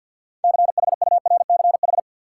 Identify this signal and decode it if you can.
Morse code, heard as a keyed tone, about 53 wpm, 700 Hz. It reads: CHURCH
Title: Six Letter Words - 50wpm